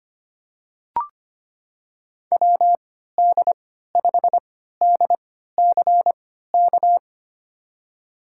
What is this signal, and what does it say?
Morse code, heard as a keyed tone, about 25 wpm, spaced as for 14 wpm, 700 Hz. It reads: WD5DCK
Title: USA Call Signs - 6 Characters - 3x Character Spacing - 25wpm